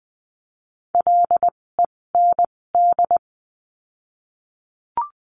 Morse code, heard as a keyed tone, about 20 wpm, spaced as for 16 wpm, 700 Hz. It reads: LEND
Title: 256 Lesson 42a - Words 12-20wpm